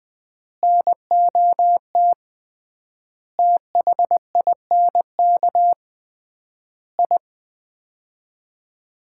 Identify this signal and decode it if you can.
Morse code, heard as a keyed tone, about 21 wpm, 700 Hz. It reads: NOT THINK I
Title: 3x Word-Spacing - Sets of 5 Words - 20wpm